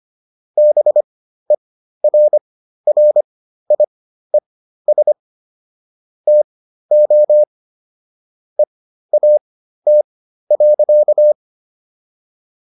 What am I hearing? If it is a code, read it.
Morse code, heard as a keyed tone, about 25 wpm, spaced as for 13 wpm, 600 Hz. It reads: BERRIES TO EAT.